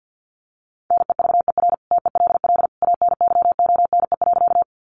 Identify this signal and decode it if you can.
Morse code, heard as a keyed tone, about 49 wpm, 700 Hz. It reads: NEVER TELL ANYONE.
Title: Sentences from Top 900 Words 50wpm